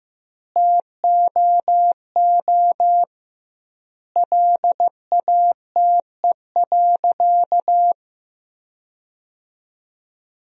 Morse code, heard as a keyed tone, about 15 wpm, 700 Hz. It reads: TOO LATE.
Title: 2x Word-Spacing - Sentences from Top 300 Words - 15wpm